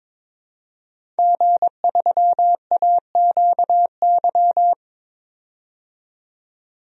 Morse code, heard as a keyed tone, about 22 wpm, 700 Hz. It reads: G3AQY
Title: DXCC-100 Call Signs - 5 Characters - 22wpm